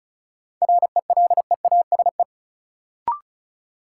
Morse code, heard as a keyed tone, about 34 wpm, 700 Hz. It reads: RELEASE
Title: Sentences from Top 1100 Words - New Word Drill Alt - 35wpm